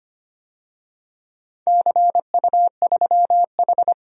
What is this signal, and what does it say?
Morse code, heard as a keyed tone, about 25 wpm, 700 Hz. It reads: CU3H
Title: DXCC-100 Call Signs - 4 Characters - 25wpm